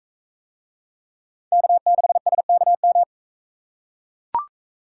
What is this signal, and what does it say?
Morse code, heard as a keyed tone, about 41 wpm, 700 Hz. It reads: K6SKM